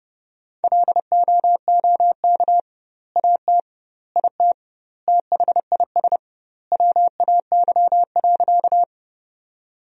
Morse code, heard as a keyed tone, about 31 wpm, 700 Hz. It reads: LOOK AT IT THIS WAY.
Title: 2x Word-Spacing - Sentences from Top 100 Words - 30wpm